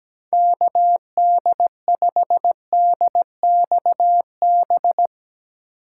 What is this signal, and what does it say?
Morse code, heard as a keyed tone, about 18 wpm, 700 Hz. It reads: KD5DXB
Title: Speed Racing - USA 2x3 Call Signs 17wpm